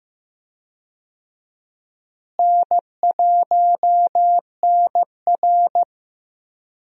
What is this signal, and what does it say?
Morse code, heard as a keyed tone, about 16 wpm, 700 Hz. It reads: N1NR
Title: USA Call Signs - 4 Characters - Repeated 2x - 15wpm